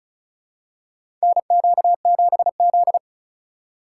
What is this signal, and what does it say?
Morse code, heard as a keyed tone, about 36 wpm, 700 Hz. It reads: NQ7Z
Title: USA Call Signs - 4 Characters - 3x Character Spacing - 35wpm